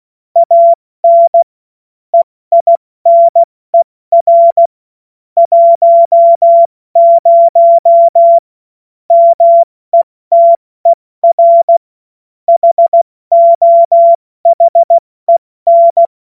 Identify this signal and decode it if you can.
Morse code, heard as a keyed tone, about 16 wpm, 680 Hz. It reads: AN EINER 10 METER HOHEN
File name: Morsekurs / Witze / Witz 07.November 2023 70 F80